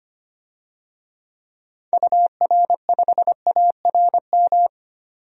Morse code, heard as a keyed tone, about 25 wpm, 700 Hz. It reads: UR5ARM
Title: DXCC-100 Call Signs - 6 Characters - 25wpm